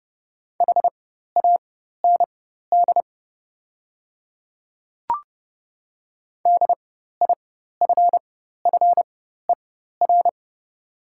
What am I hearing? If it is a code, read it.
Morse code, heard as a keyed tone, about 30 wpm, spaced as for 14 wpm, 700 Hz. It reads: HAND DIFFER
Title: 4x Character Spacing - Top 200 Words - 30wpm